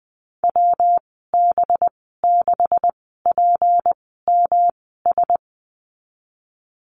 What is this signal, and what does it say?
Morse code, heard as a keyed tone, about 20 wpm, spaced as for 15 wpm, 700 Hz. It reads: WB6PMS
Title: 193 Lesson 31g - Letter v - Call Signs 20wpm